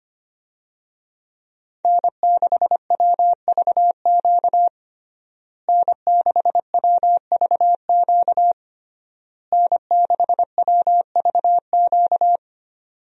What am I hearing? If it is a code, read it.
Morse code, heard as a keyed tone, about 25 wpm, 700 Hz. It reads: N6WVQ N6WVQ N6WVQ
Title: USA Call Signs - 5 Characters - Repeated 3x - 25wpm